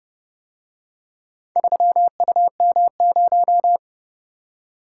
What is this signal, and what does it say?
Morse code, heard as a keyed tone, about 29 wpm, 700 Hz. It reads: 3UM0